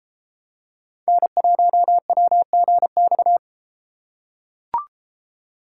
Morse code, heard as a keyed tone, about 33 wpm, 700 Hz. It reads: N1WGX